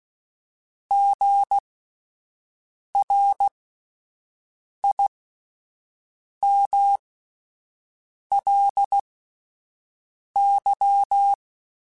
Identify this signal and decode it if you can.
Morse code, heard as a keyed tone, about 16 wpm, spaced as for 5 wpm, 800 Hz. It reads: GRIMLY
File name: Spaced Alphas / Spaced Alphas mp3s / Gs sp @10